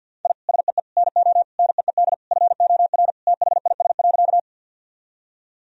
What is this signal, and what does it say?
Morse code, heard as a keyed tone, about 51 wpm, 700 Hz. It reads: I SEE NO NEED FOR THIS.
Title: Sentences from Top 1000 Words - Review Part 1 - 50wpm